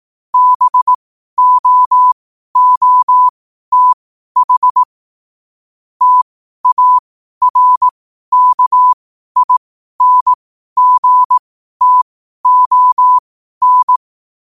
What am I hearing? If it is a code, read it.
Morse code, heard as a keyed tone, about 18 wpm, spaced as for 12 wpm, 1000 Hz. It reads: BOOTH TARKINGTON
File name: qotdfiles / 12wpm